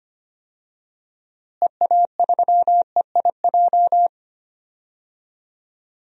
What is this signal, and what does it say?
Morse code, heard as a keyed tone, about 25 wpm, 700 Hz. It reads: EA3EIJ
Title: DXCC-100 Call Signs - 6 Characters - Repeated 3x - 25wpm